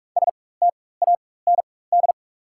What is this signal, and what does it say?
Morse code, heard as a keyed tone, about 42 wpm, spaced as for 20 wpm, 690 Hz. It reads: STAND
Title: 4x Character Spacing - Top 500 Words - 45wpm